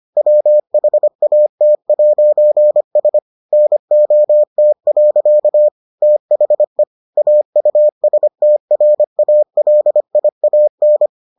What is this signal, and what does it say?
Morse code, heard as a keyed tone, about 25 wpm, 600 Hz. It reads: WHAT'S NOT. THE AUSTRALIAN